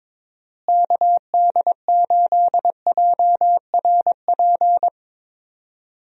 Morse code, heard as a keyed tone, about 22 wpm, 700 Hz. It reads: KD8JRP
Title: USA Call Signs - 6 Characters - Repeated 3x - 22wpm